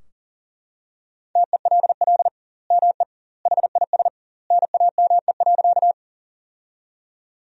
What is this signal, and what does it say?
Morse code, heard as a keyed tone, about 40 wpm, 700 Hz. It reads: TELL ME HIS NAME.
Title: 2x Word-Spacing - Sentences from Top 200 Words - 40wpm